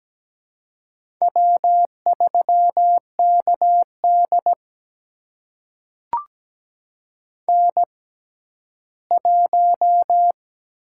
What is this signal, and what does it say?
Morse code, heard as a keyed tone, about 17 wpm, spaced as for 12 wpm, 700 Hz. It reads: W3KD N1
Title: USA Call Signs - 4 Characters - 6x Character Spacing - 17wpm